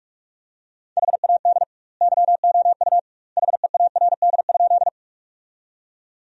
Morse code, heard as a keyed tone, about 46 wpm, 700 Hz. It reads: HAD YOU HEARD?
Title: 2x Word-Spacing - Sentences from Top 400 Words - 45wpm